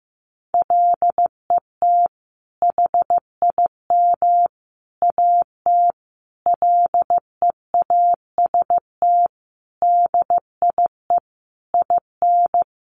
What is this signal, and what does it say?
Morse code, heard as a keyed tone, about 15 wpm, 700 Hz. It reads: LET HIM AT LEAST DIE IN
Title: Sentences from Top 1500 Words 15wpm